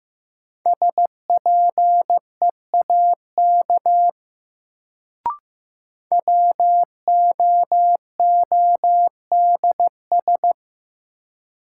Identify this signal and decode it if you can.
Morse code, heard as a keyed tone, about 15 wpm, 700 Hz. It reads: SPEAK WOODS
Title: Sentences from Top 1000 Words - Review Part 2 - New Word Drill - 15wpm